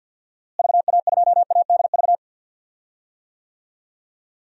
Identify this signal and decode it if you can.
Morse code, heard as a keyed tone, about 46 wpm, 700 Hz. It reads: VA2ADV
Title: DXCC-100 Call Signs - 6 Characters - Repeated 3x - 50wpm